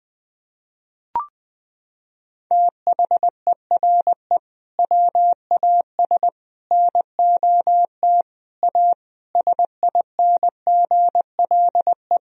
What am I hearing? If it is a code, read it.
Morse code, heard as a keyed tone, about 20 wpm, 700 Hz. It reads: THERE WAS NOT A SINGLE
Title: Sentences from Top 1000 Words - Review Part 4 - 20wpm